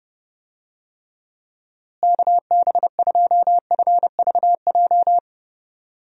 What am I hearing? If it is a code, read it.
Morse code, heard as a keyed tone, about 32 wpm, 700 Hz. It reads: KB2FVJ
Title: USA Call Signs - 6 Characters - Repeated 2x - 30wpm